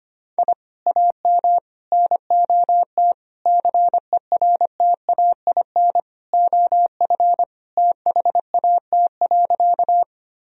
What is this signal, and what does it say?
Morse code, heard as a keyed tone, about 25 wpm, 700 Hz. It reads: I AM NOT CERTAIN OF THAT.
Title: Sentences from Top 1000 Words - Review Part 2 - 25wpm